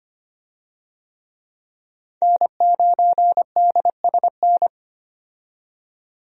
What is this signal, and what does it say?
Morse code, heard as a keyed tone, about 25 wpm, 700 Hz. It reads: N9DSN
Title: USA Call Signs - 5 Characters - 25wpm